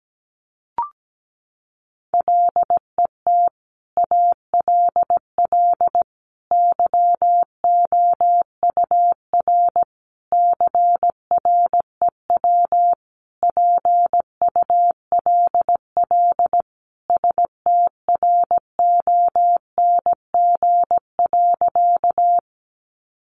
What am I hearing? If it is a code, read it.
Morse code, heard as a keyed tone, about 17 wpm, 700 Hz. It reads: LET ALL YOUR CREW PULL STRONG.
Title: Sentences from Top 1500 Words 17wpm